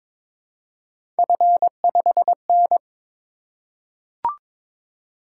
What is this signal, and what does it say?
Morse code, heard as a keyed tone, about 22 wpm, 700 Hz. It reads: F5N
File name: practice / 22wpm / Three Letter-Number Combinations - 22wpm